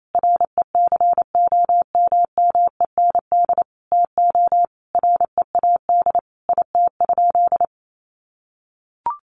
Morse code, heard as a keyed tone, about 28 wpm, 700 Hz. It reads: RECOMMEND TO READ IT?
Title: Sentences from Top 1200 Words 28wpm